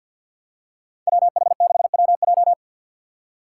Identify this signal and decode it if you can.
Morse code, heard as a keyed tone, about 53 wpm, 700 Hz. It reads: WH6WJ
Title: USA Call Signs - 5 Characters - 3x Character Spacing - 50wpm